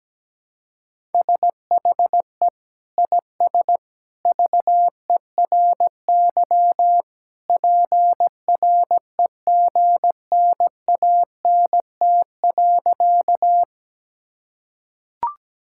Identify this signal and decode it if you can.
Morse code, heard as a keyed tone, about 17 wpm, 700 Hz. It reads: SHE IS VERY PREGNANT.
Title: Sentences from Top 1000 Words - Review Part 6 - 17wpm